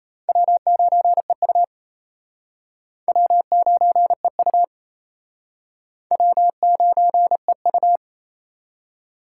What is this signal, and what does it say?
Morse code, heard as a keyed tone, about 34 wpm, 700 Hz. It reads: W9EU W9EU W9EU
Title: Speed Racing - USA 1x2 Call Signs 28wpm